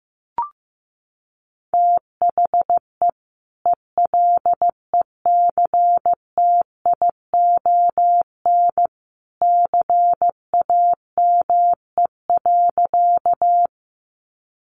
Morse code, heard as a keyed tone, about 15 wpm, 700 Hz. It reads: THE ELECTION CAME.
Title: Sentences from Top 1500 Words 15wpm